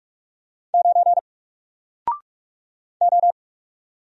Morse code, heard as a keyed tone, about 45 wpm, 700 Hz. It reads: 9 O
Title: Single Letter-Number 45wpm